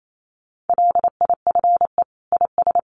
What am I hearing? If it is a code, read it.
Morse code, heard as a keyed tone, about 28 wpm, 700 Hz. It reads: LIFE IS
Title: Sentences from Top 700 Words 28wpm